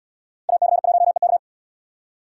Morse code, heard as a keyed tone, about 71 wpm, 700 Hz. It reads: AC1EK